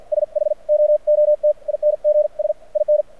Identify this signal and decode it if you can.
Morse code, heard as a keyed tone, about 54 wpm, spaced as for 42 wpm, 600 Hz. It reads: SHOOTINGS IN
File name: archived / CRSnet